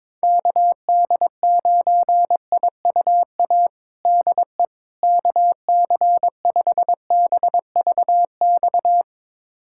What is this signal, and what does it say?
Morse code, heard as a keyed tone, about 22 wpm, 700 Hz. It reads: KD9IUA DE KC5BVX
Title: USA Callsign DE Callsign 22wpm